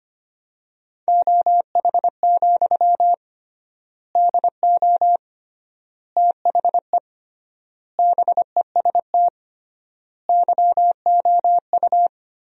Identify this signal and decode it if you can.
Morse code, heard as a keyed tone, about 25 wpm, 700 Hz. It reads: OH, DO THE BEST YOU
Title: 3x Word-Spacing - Sentences from Top 500 Words - 25wpm